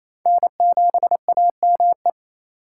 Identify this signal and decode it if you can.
Morse code, heard as a keyed tone, about 28 wpm, 700 Hz. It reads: N7AME
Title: K1USN SST QSO Practice - 28wpm